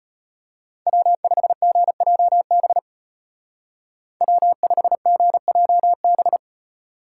Morse code, heard as a keyed tone, about 30 wpm, 700 Hz. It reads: W5GJB W5GJB
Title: Speed Racing - USA 1x3 Call Signs 25wpm